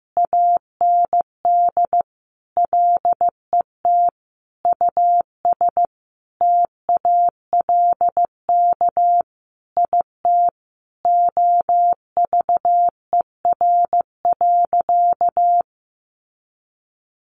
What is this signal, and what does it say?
Morse code, heard as a keyed tone, about 15 wpm, 700 Hz. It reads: AND LET US TALK IT OVER.
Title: Sentences from Top 500 Words 15wpm